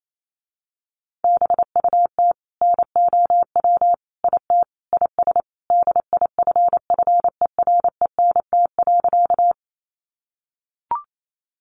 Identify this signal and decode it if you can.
Morse code, heard as a keyed tone, about 28 wpm, 700 Hz. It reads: BUT NOW IT IS DIFFERENT.
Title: Sentences from Top 600 Words 28wpm